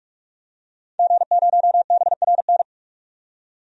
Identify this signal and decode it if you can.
Morse code, heard as a keyed tone, about 43 wpm, 700 Hz. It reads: G0BRN
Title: DXCC-100 Call Signs - 5 Characters - 45wpm